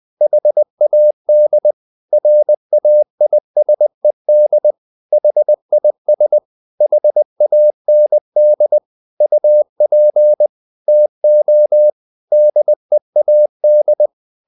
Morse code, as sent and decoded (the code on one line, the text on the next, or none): .... .- -.. / .-. .- .. ... -.. / .... .. ... / .... .- -. -.. / ..- .--. / - --- / -.. .- -..
HAD RAISED HIS HAND UP TO DEAD